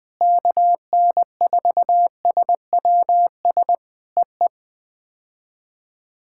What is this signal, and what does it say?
Morse code, heard as a keyed tone, about 21 wpm, 700 Hz. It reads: KN4SWS EE
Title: POTA QSO Practice - 20wpm